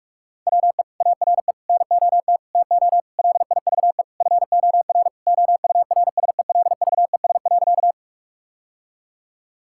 Morse code, heard as a keyed tone, about 48 wpm, 700 Hz. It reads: WE ARE NOT TO LIVE FOR OURSELVES.
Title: Sentences from Top 1600 Words 45wpm